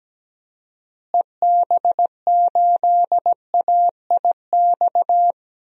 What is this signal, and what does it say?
Morse code, heard as a keyed tone, about 17 wpm, 700 Hz. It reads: EB8AIX
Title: DXCC-100 Call Signs - 6 Characters - Repeated 3x - 17wpm